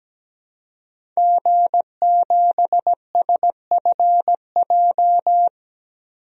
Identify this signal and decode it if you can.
Morse code, heard as a keyed tone, about 17 wpm, 700 Hz. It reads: G7SFJ
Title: DXCC-100 Call Signs - 5 Characters - 17wpm